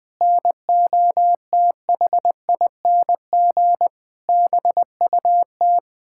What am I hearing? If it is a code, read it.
Morse code, heard as a keyed tone, about 21 wpm, 700 Hz. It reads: NOTHING BUT
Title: Sentences with 9-Word Phrases - Spoken Beginning - 20wpm